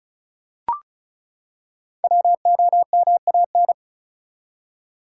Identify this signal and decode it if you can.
Morse code, heard as a keyed tone, about 36 wpm, 700 Hz. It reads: WOMAN
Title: Five Letter Words - 35wpm